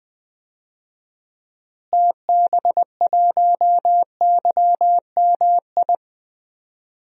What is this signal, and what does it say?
Morse code, heard as a keyed tone, about 20 wpm, 700 Hz. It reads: TB1YMI
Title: DXCC-100 Call Signs - 6 Characters - Repeated 2x - 20wpm